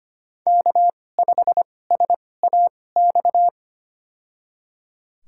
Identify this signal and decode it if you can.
Morse code, heard as a keyed tone, about 25 wpm, spaced as for 18 wpm, 700 Hz. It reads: K5SAX
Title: USA Call Signs - 5 Characters - 2x Character Spacing - 25wpm